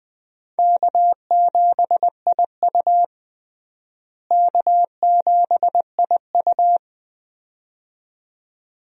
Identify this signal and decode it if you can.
Morse code, heard as a keyed tone, about 20 wpm, 700 Hz. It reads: K7IU K7IU
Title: USA Call Signs - 4 Characters - Repeated 2x - 20wpm